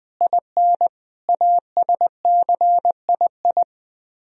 Morse code, heard as a keyed tone, about 20 wpm, 700 Hz. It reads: IN ASCII